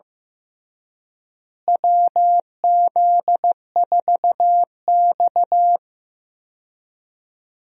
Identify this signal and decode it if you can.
Morse code, heard as a keyed tone, about 15 wpm, 700 Hz. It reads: WZ4X
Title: USA Call Signs - 4 Characters - Repeated 3x - 15wpm